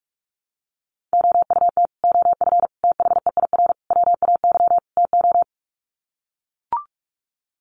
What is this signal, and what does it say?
Morse code, heard as a keyed tone, about 44 wpm, 700 Hz. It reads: OUT OF THEIR WAY TO
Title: Sets of 5 Words Encore2 45wpm